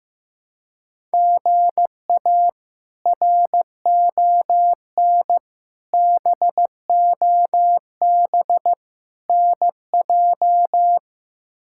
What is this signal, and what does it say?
Morse code, heard as a keyed tone, about 15 wpm, 700 Hz. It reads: GA RON BOB NJ
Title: K1USN SST QSO Practice - 15wpm